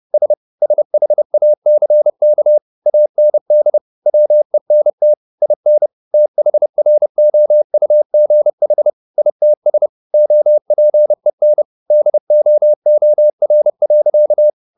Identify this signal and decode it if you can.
Morse code, heard as a keyed tone, about 30 wpm, 600 Hz. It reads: S SHACK AND WENT IN THROUGH ITS OPEN DOOR.
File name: the old man and the sea / the old man and the sea 1 30wpm